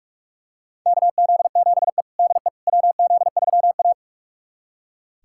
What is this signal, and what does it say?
Morse code, heard as a keyed tone, about 41 wpm, 700 Hz. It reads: KZ7E DE WZ3A
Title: USA Callsign DE Callsign - 4 Characters - 45wpm